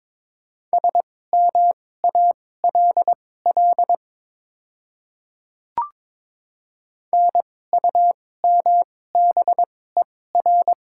Morse code, heard as a keyed tone, about 22 wpm, spaced as for 16 wpm, 700 Hz. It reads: SMALL NUMBER
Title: 2x Character Spacing - Top 200 Words - 22wpm